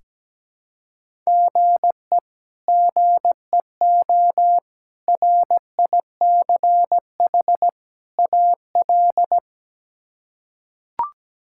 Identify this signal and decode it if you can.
Morse code, heard as a keyed tone, about 17 wpm, 700 Hz. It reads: GE GEO RICH AL
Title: K1USN SST QSO Practice - 17wpm